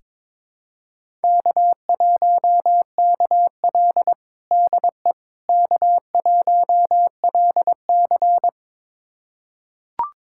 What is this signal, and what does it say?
Morse code, heard as a keyed tone, about 22 wpm, 700 Hz. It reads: K1KL DE K1LC
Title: USA Callsign DE Callsign - 4 Characters - 22wpm